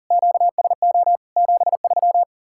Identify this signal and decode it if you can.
Morse code, heard as a keyed tone, about 48 wpm, 700 Hz. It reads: QSO 73
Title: Speed-Racing Common QSO Phrases - 40wpm